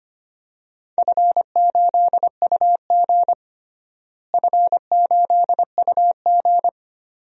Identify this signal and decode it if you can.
Morse code, heard as a keyed tone, about 25 wpm, 700 Hz. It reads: F8UG F8UG
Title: DXCC-100 Call Signs - 4 Characters - Repeated 2x - 25wpm